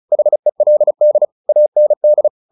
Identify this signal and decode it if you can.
Morse code, heard as a keyed tone, about 34 wpm, 600 Hz. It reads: HELD AND
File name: the old man and the sea / the old man and the sea 1 35wpm